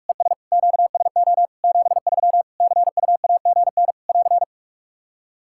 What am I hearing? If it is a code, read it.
Morse code, heard as a keyed tone, about 46 wpm, 700 Hz. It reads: ES QSO 73 CUAGN <AR>
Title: Practice QSO - Exchange 4 - Ending - 45wpm